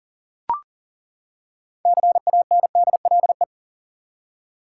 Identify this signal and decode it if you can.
Morse code, heard as a keyed tone, about 40 wpm, 700 Hz. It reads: CANDLE
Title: Six Letter Words - 40wpm